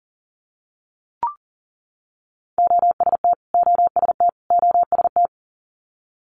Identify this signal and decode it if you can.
Morse code, heard as a keyed tone, about 37 wpm, 690 Hz. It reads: OST OST OST
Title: Multi-letter Phonemes Repeated 3x 40wpm